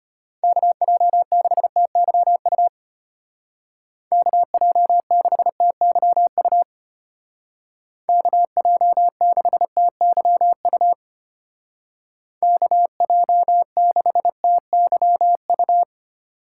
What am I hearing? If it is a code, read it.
Morse code, heard as a keyed tone, about 32 wpm, 700 Hz. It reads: KJ6TYU KJ6TYU KJ6TYU KJ6TYU